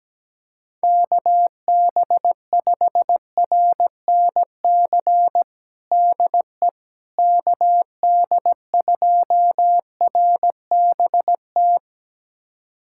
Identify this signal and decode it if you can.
Morse code, heard as a keyed tone, about 17 wpm, 700 Hz. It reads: KB5RNC DE KD2RBT
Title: USA Callsign DE Callsign - 6 Characters - 17wpm